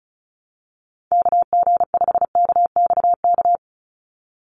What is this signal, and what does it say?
Morse code, heard as a keyed tone, about 34 wpm, 700 Hz. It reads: KG5KXK
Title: USA Call Signs 35wpm